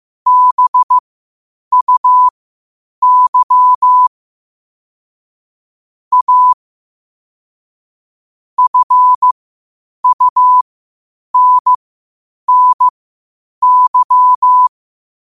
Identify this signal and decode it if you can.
Morse code, heard as a keyed tone, about 15 wpm, spaced as for 8 wpm, 1000 Hz. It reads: BUY A FUNNY
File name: qotdfiles / 8wpm